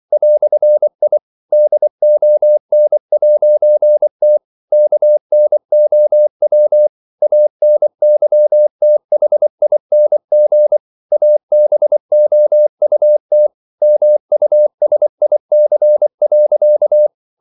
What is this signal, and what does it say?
Morse code, heard as a keyed tone, about 24 wpm, 600 Hz. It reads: "I DON'T KNOW ANYTHING ABOUT MUSIC.
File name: CW / Practice / EnglishQuotations0324